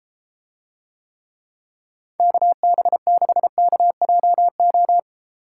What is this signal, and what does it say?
Morse code, heard as a keyed tone, about 32 wpm, 700 Hz. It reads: KB6KJO